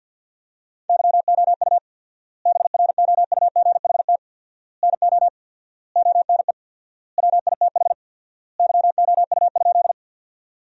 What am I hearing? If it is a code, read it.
Morse code, heard as a keyed tone, about 50 wpm, 700 Hz. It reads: YOU BROUGHT NO ONE WITH YOU?
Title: 4x Word-Spacing - Sentences from Top 500 Words - 50wpm